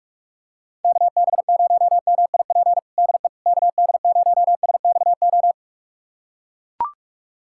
Morse code, heard as a keyed tone, about 41 wpm, 700 Hz. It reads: KB0MIP DE KD0SXO
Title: USA Callsign DE Callsign 45wpm